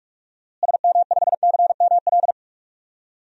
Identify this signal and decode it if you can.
Morse code, heard as a keyed tone, about 46 wpm, 700 Hz. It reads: SM5CML
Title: DXCC-100 Call Signs - 6 Characters - Repeated 3x - 45wpm